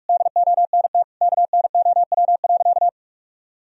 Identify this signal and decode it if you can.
Morse code, heard as a keyed tone, about 46 wpm, 700 Hz. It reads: DONT KNOW.
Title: Sentences from Top 1000 Words - Review Part 3 - 45wpm